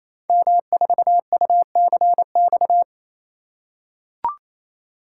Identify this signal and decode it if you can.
Morse code, heard as a keyed tone, about 27 wpm, 700 Hz. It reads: M4UCX
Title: Code Groups - 5 Letter:Numbers - 28wpm